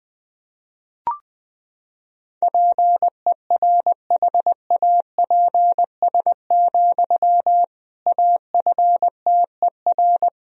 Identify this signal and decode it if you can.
Morse code, heard as a keyed tone, about 21 wpm, 700 Hz. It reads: PERHAPS, AFTER
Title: Sentences from Top 500 Words and Abbreviations 20wpm